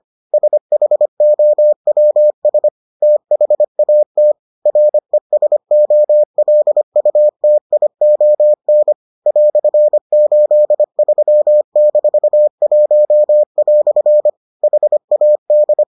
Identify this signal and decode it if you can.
Morse code, heard as a keyed tone, about 25 wpm, 600 Hz. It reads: SHOWS THAT RESOLUTION "83-1" HAD